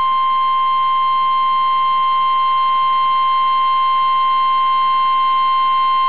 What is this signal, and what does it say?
Emergency Alert System (SAME) tones (the 853 and 960 Hz attention tone). It is heard from 1.1 to 5.7 s.